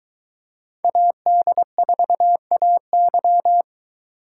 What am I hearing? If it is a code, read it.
Morse code, heard as a keyed tone, about 22 wpm, 700 Hz. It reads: AD4AY